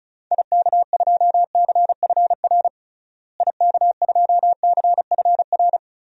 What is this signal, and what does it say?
Morse code, heard as a keyed tone, about 34 wpm, 700 Hz. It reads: IK2CFR IK2CFR
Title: DXCC-100 Call Signs - 6 Characters - Repeated 2x - 35wpm